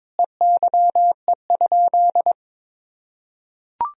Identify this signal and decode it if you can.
Morse code, heard as a keyed tone, about 22 wpm, 700 Hz. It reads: EYE?
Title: Sentences from Top 1000 Words - Review Part 1 - 22wpm